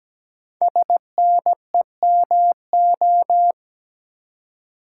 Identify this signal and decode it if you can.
Morse code, heard as a keyed tone, about 17 wpm, 700 Hz. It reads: SNEMO
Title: Code Groups - 5 Letters - 17wpm